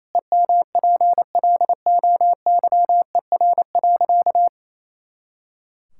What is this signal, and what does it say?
Morse code, heard as a keyed tone, about 28 wpm, 700 Hz. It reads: EMPLOYER.
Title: Pangram Challenge v2 - 28wpm